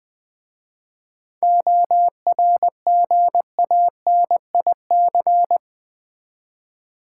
Morse code, heard as a keyed tone, about 20 wpm, 700 Hz. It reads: ORGANIC
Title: 7 Letter Words - First 3 Letters Given - 20wpm